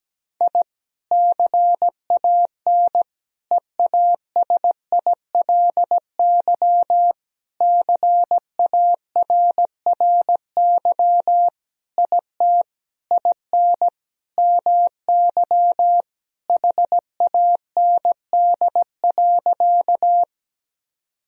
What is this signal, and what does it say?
Morse code, heard as a keyed tone, about 17 wpm, 700 Hz. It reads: I CAN EASILY CARRY IT IN MY HAND.
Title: Sentences from Top 1000 Words - Review Part 1 - 17wpm